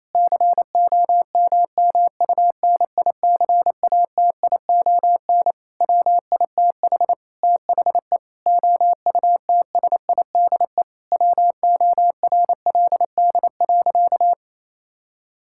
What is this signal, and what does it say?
Morse code, heard as a keyed tone, about 27 wpm, 700 Hz. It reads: COMMUNICATION WITH THE OUTSIDE WORLD.
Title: Sentences from Top 1000 Words - Review Part 6 - 28wpm